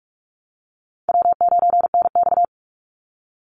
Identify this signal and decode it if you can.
Morse code, heard as a keyed tone, about 44 wpm, 700 Hz. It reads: W9NX
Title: Easy Call Signs Repeated 2x - 45wpm